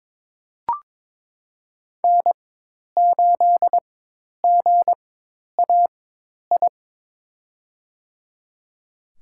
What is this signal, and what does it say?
Morse code, heard as a keyed tone, about 21 wpm, spaced as for 10 wpm, 700 Hz. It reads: N8GAI
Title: USA Call Signs - 5 Characters - 4x Character Spacing - 22wpm